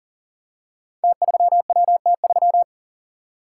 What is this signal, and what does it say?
Morse code, heard as a keyed tone, about 40 wpm, 700 Hz. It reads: T3WT3
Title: Code Groups - 5 Letter:Numbers - 40wpm